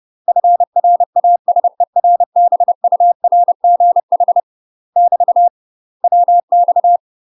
Morse code, heard as a keyed tone, about 30 wpm, 700 Hz. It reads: FRASERBURGH <BT> WX
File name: UK MASTER QSOs TRAINING / 30WPM / UK MASTER QSOs 0033